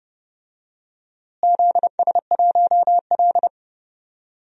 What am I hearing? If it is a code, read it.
Morse code, heard as a keyed tone, about 30 wpm, 700 Hz. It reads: ZS1L
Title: DXCC-100 Call Signs - 4 Characters - Repeated 2x - 30wpm